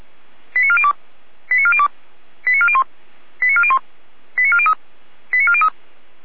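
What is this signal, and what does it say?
Five-tone selective call (ZVEI), heard as quick runs of five tones, tones of about 70 ms. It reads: 89402 89402 89401 89401 89403 89403
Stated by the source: ZVEI five-tone call received by SDR.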